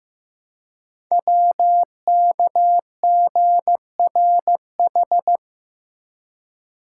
Morse code, heard as a keyed tone, about 15 wpm, 700 Hz. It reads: WKGRH
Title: Code Groups - 5 Letters - 15wpm